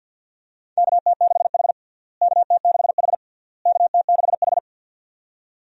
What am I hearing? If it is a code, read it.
Morse code, heard as a keyed tone, about 52 wpm, 700 Hz. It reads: KT6H KT6H KT6H
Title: USA Call Signs - 4 Characters - Repeated 3x - 50wpm